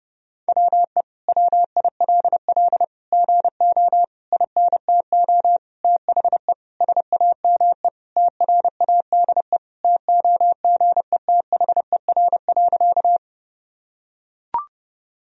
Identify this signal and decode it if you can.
Morse code, heard as a keyed tone, about 30 wpm, 700 Hz. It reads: WE WILL GO INTO THE SAME TRADE TOGETHER.
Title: Sentences from Top 1000 Words - Review Part 4 - 30wpm